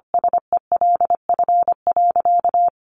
Morse code, heard as a keyed tone, about 25 wpm, 700 Hz. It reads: SELF.
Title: Sentences from Top 1300 Words 25wpm